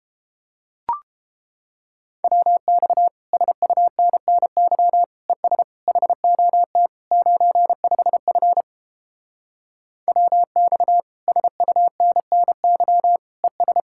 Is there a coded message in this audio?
Yes: Morse code from 2.2 to 13.8 s, heard as a keyed tone, about 31 wpm, 700 Hz. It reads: WX SUNNY ES HOT 95F WX SUNNY ES